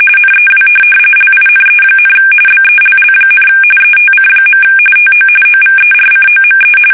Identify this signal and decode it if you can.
RTTY, about 45.45 baud. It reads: NEW LINE, BELL, ETC. TO OVERCOME
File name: fsk850 / teletype-RTTY